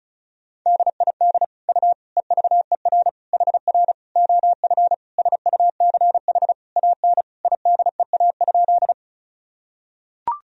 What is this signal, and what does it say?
Morse code, heard as a keyed tone, about 36 wpm, 700 Hz. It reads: DID U EVER HR OF SUCH AN IDEA?
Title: Sentences from Top 400 Words and Abbreviations 35wpm